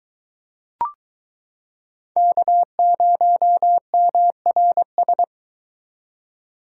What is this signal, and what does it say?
Morse code, heard as a keyed tone, about 22 wpm, 700 Hz. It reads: K0MRS